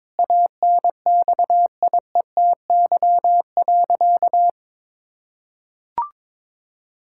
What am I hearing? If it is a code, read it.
Morse code, heard as a keyed tone, about 22 wpm, 700 Hz. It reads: ANXIETY.
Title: Sentences from Top 1600 Words 22wpm